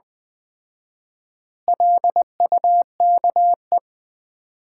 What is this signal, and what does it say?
Morse code, heard as a keyed tone, about 20 wpm, 700 Hz. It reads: LUKE
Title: Common QSO Phrases - 4x Word-Spacing - 20wpm